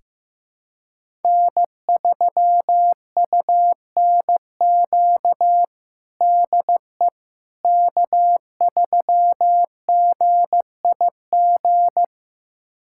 Morse code, heard as a keyed tone, about 15 wpm, 700 Hz. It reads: N3UNQ DE K3GIG
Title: USA Callsign DE Callsign - 5 Characters - 15wpm